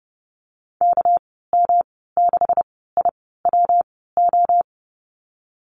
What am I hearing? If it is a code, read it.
Morse code, heard as a keyed tone, about 32 wpm, spaced as for 17 wpm, 700 Hz. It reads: KM6IWO
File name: Lesson 37g - Letter x - Call Signs 30wpm